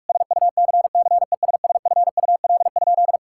Morse code, heard as a keyed tone, about 46 wpm, 700 Hz. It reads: SUCCESSFUL?
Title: Sentences from Top 1000 Words - Review Part 5 - 45wpm